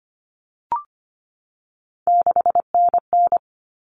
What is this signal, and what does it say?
Morse code, heard as a keyed tone, about 25 wpm, 700 Hz. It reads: 6NN
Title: Common Cut Numbers - 3-Digit 25wpm